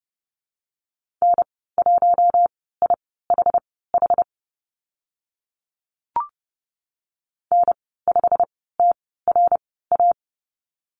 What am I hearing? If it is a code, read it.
Morse code, heard as a keyed tone, about 30 wpm, spaced as for 17 wpm, 700 Hz. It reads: N1IHH N5TRA
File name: Lesson 15g - Number 5 - Call Signs 30wpm